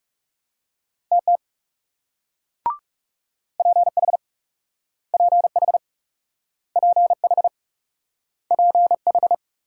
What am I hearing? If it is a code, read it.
Morse code, heard as a keyed tone, about 41 wpm, 700 Hz. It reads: TT PH PH PH PH